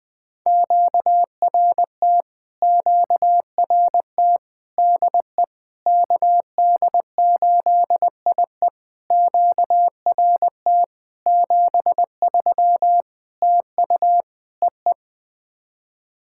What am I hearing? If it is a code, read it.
Morse code, heard as a keyed tone, about 20 wpm, 700 Hz. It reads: QRT QRT DE KD8IE QRT 73 TU EE
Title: POTA QSO Practice - 20wpm